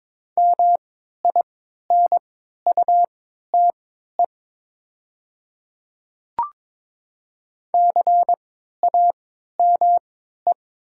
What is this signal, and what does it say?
Morse code, heard as a keyed tone, about 22 wpm, spaced as for 12 wpm, 700 Hz. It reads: MINUTE CAME
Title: 3x Character Spacing - Top 500 Words - 22wpm